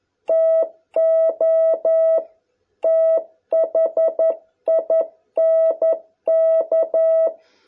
Morse code, heard as a keyed tone, about 11 wpm, 660 Hz. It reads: TO THINK